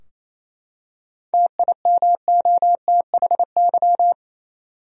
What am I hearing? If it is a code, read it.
Morse code, heard as a keyed tone, about 28 wpm, 700 Hz. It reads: TIMOTHY
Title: USA Common Names - 28wpm